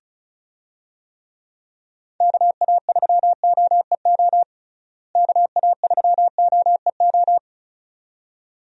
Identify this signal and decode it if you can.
Morse code, heard as a keyed tone, about 34 wpm, 700 Hz. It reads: KA3OEO KA3OEO
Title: USA Call Signs - 6 Characters - Repeated 2x - 35wpm